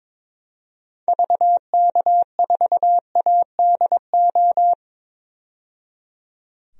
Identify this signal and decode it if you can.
Morse code, heard as a keyed tone, about 22 wpm, 700 Hz. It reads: VK4ADO
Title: DXCC-100 Call Signs - 6 Characters - 22wpm